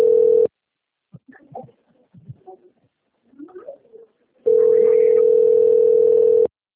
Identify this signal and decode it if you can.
Ringback tone (North America), 440 + 480 Hz.